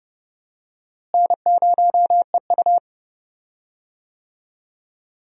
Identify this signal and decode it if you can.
Morse code, heard as a keyed tone, about 30 wpm, 700 Hz. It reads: N0EU
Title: USA Call Signs - 4 Characters - Repeated 3x - 30wpm